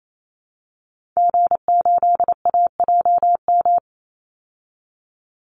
Morse code, heard as a keyed tone, about 28 wpm, 700 Hz. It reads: G8AJM
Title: Call Signs Repeated 2x - 28wpm